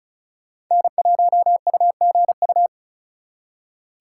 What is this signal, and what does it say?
Morse code, heard as a keyed tone, about 39 wpm, 700 Hz. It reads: N1UGU
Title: Speed Racing - USA 1x3 Call Signs 35wpm